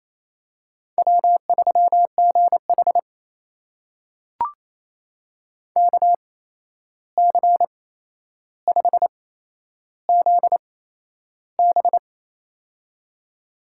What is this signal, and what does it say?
Morse code, heard as a keyed tone, about 27 wpm, spaced as for 11 wpm, 700 Hz. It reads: W3GH KC5ZB
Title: USA Call Signs - 8x Character Spacing - 28wpm